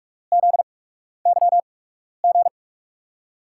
Morse code, heard as a keyed tone, about 44 wpm, spaced as for 12 wpm, 690 Hz. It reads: ZYG
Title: USA Call Signs - 8x Character Spacing - 45wpm